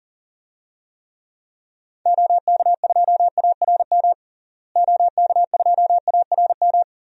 Morse code, heard as a keyed tone, about 40 wpm, 700 Hz. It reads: OK2ARM OK2ARM
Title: DXCC-100 Call Signs - 6 Characters - Repeated 3x - 40wpm